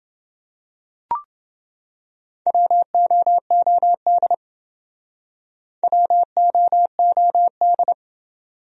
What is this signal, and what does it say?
Morse code, heard as a keyed tone, about 28 wpm, 700 Hz. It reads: WOOD WOOD